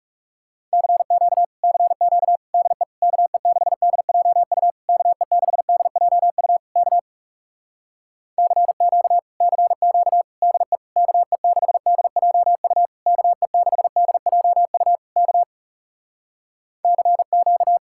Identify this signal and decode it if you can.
Morse code, heard as a keyed tone, about 37 wpm, 700 Hz. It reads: CQ CQ DE KE6DJU KE6DJU K CQ CQ DE KE6DJU KE6DJU K CQ
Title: Speed-Racing Common QSO Phrases - 30wpm